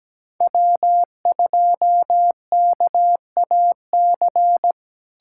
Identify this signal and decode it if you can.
Morse code, heard as a keyed tone, about 17 wpm, 700 Hz. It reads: W2KAC
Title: USA Callsign DE Callsign - 5 Characters - 17wpm